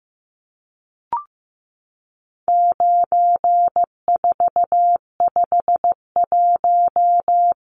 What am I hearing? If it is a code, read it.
Morse code, heard as a keyed tone, about 15 wpm, 700 Hz. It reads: 9451
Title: Numbers - 4-Digit 15wpm